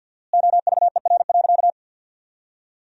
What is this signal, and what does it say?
Morse code, heard as a keyed tone, about 50 wpm, 700 Hz. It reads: OVER.
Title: 4x Word-Spacing - Sentences from Top 200 Words - 50wpm